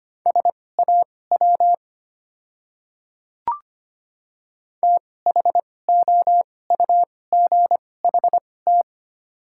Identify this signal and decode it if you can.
Morse code, heard as a keyed tone, about 25 wpm, spaced as for 18 wpm, 700 Hz. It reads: SAW THOUGHT
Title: 2x Character Spacing - Top 300 Words - 25wpm